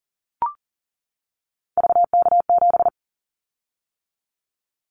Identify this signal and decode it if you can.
Morse code, heard as a keyed tone, about 40 wpm, 700 Hz. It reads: VK7
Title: Callsign Prefixes for Top 100 DXCC - 40wpm